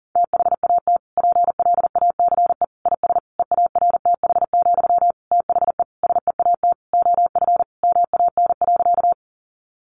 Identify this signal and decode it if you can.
Morse code, heard as a keyed tone, about 40 wpm, 700 Hz. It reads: THAT PLACE IS EARTH, THE SEAT OF MAN.
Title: Sentences from Top 1500 Words 40wpm